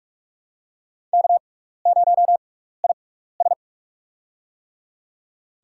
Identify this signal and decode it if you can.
Morse code, heard as a keyed tone, about 43 wpm, spaced as for 15 wpm, 700 Hz. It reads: K0IS
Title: USA Call Signs - 4 Characters - 6x Character Spacing - 45wpm